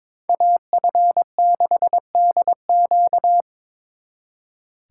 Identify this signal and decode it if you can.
Morse code, heard as a keyed tone, about 27 wpm, 700 Hz. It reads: AF6DQ
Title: Speed Racing - USA 2x2 Call Signs 22wpm